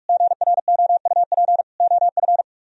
Morse code, heard as a keyed tone, about 53 wpm, 700 Hz. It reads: GROUP OF
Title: Speed-Racing - Sets of 3 Words 45wpm